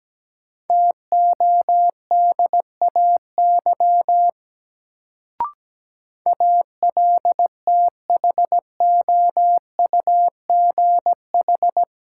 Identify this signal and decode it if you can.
Morse code, heard as a keyed tone, about 17 wpm, 700 Hz. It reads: TODAY ALTHOUGH
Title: Sentences from Top 600 Words - New Word Drill - 17wpm